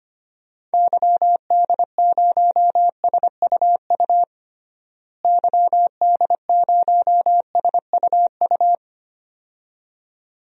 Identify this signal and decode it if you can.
Morse code, heard as a keyed tone, about 25 wpm, 700 Hz. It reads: YD0SUU YD0SUU
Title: DXCC-100 Call Signs - 6 Characters - Repeated 2x - 25wpm